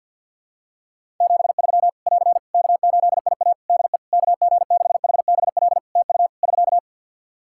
Morse code, heard as a keyed tone, about 51 wpm, 700 Hz. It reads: 73 <AR> K8IA DE KG6HBL TU <SK>
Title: Practice QSO - Exchange 4 - Ending - 50wpm